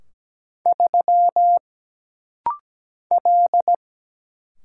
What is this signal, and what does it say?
Morse code, heard as a keyed tone, about 17 wpm, 700 Hz. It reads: 3 L
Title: Single Letter-Number Alt - Rapid-Fire 17wpm